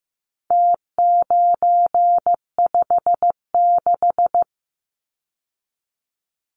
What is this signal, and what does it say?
Morse code, heard as a keyed tone, about 15 wpm, 700 Hz. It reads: T956
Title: Numbers - Contest Style - 4-Digit 15wpm